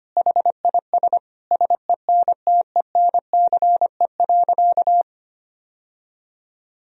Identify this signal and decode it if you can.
Morse code, heard as a keyed tone, about 25 wpm, 700 Hz. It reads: HIS SENTENCE.
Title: Sentences from Top 1000 Words - Review Part 1 - 25wpm